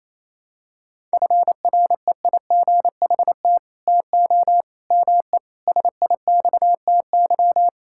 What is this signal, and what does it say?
Morse code, heard as a keyed tone, about 28 wpm, 700 Hz. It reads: FREIGHT TO ME SIXTY
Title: Pangram Challenge v2 - 28wpm